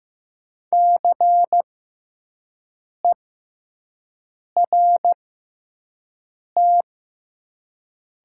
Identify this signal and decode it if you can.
Morse code, heard as a keyed tone, about 16 wpm, spaced as for 5 wpm, 700 Hz. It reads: CERT
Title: 6x Character Spacing - Top 500 Words - 15wpm